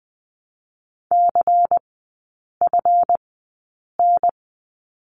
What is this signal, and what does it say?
Morse code, heard as a keyed tone, about 20 wpm, spaced as for 8 wpm, 690 Hz. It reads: CFN
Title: 159 Lesson 26e - Letter g - Three Characters 20wpm